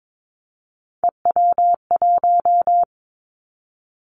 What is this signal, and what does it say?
Morse code, heard as a keyed tone, about 22 wpm, 700 Hz. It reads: EW1
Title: Three Challenging Letter-Number Combinations 22wpm